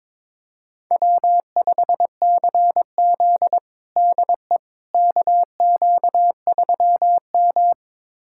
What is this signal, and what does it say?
Morse code, heard as a keyed tone, about 22 wpm, 700 Hz. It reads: W5CZ DE KQ3M
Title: USA Callsign DE Callsign - 4 Characters - 22wpm